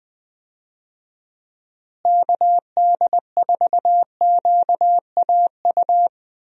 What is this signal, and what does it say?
Morse code, heard as a keyed tone, about 20 wpm, 700 Hz. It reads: KD4QAU
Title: USA Call Signs - 6 Characters - Repeated 3x - 20wpm